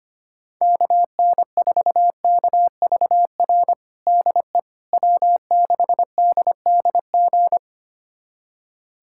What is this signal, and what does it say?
Morse code, heard as a keyed tone, about 25 wpm, 700 Hz. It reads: KN4KVR DE W6DDG
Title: USA Callsign DE Callsign 25wpm